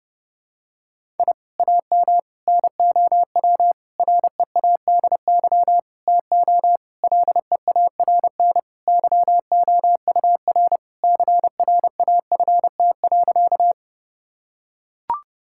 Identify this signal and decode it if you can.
Morse code, heard as a keyed tone, about 30 wpm, 700 Hz. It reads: I AM NOW READY TO LEARN YOUR CRAFT.
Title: Sentences from Top 1000 Words - Review Part 6 - 30wpm